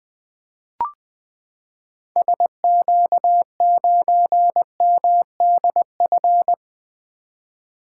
Morse code, heard as a keyed tone, about 20 wpm, 700 Hz. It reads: SQ9MDF
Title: 2023 SOTA Activator Call Signs - 20wpm